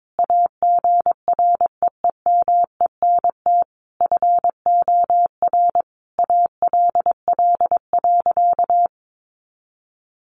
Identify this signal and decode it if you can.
Morse code, heard as a keyed tone, about 22 wpm, 700 Hz. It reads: AGREEMENT FOR ALL.
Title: Sentences from Top 1500 Words 22wpm